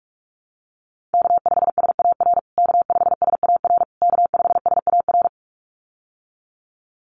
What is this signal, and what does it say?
Morse code, heard as a keyed tone, about 44 wpm, 700 Hz. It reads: K5SAR K5SAR K5SAR
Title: USA Call Signs Repeated 3x 45wpm